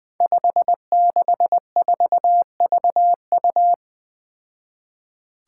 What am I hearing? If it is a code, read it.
Morse code, heard as a keyed tone, about 20 wpm, 700 Hz. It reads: 564VU
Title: Five Challenging Letter-Number Combinations 20wpm